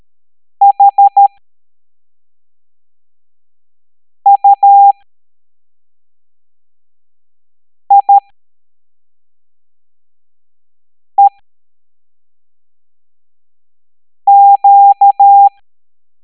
Morse code, heard as a keyed tone, about 13 wpm, spaced as for 3 wpm, 800 Hz. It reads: HUIEQ
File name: hamdata / L07-EX02-HQP